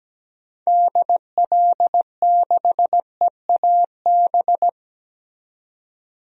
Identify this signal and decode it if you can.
Morse code, heard as a keyed tone, about 17 wpm, 700 Hz. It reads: DL6EAB
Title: DXCC-100 Call Signs - 6 Characters - 17wpm